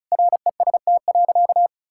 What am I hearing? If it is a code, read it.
Morse code, heard as a keyed tone, about 34 wpm, 700 Hz. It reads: REST.
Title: Sentences from Top 1000 Words - Review Part 5 - 35wpm